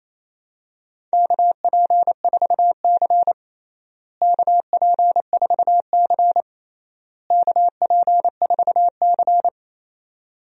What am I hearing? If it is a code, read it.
Morse code, heard as a keyed tone, about 28 wpm, 700 Hz. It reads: KP4C KP4C KP4C
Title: DXCC-100 Call Signs - 4 Characters - Repeated 3x - 28wpm